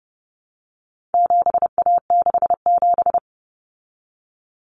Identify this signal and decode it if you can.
Morse code, heard as a keyed tone, about 30 wpm, 700 Hz. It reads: 7A67
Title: Common Cut Numbers - 4-Digit 30wpm